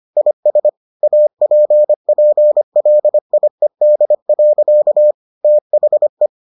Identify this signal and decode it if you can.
Morse code, heard as a keyed tone, about 25 wpm, 600 Hz. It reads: IS APPLIED. THE